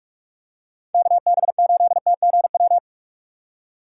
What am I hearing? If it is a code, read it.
Morse code, heard as a keyed tone, about 49 wpm, 700 Hz. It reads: KB8TGW